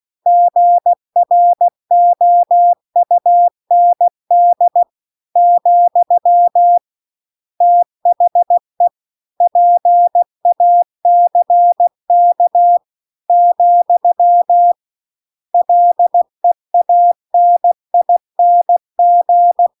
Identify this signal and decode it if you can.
Morse code, heard as a keyed tone, about 16 wpm, 700 Hz. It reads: GROUND , THE PACK , LEANING